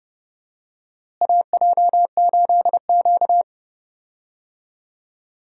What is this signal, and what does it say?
Morse code, heard as a keyed tone, about 31 wpm, 700 Hz. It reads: AJ8Q
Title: USA Call Signs - 4 Characters - Repeated 3x - 30wpm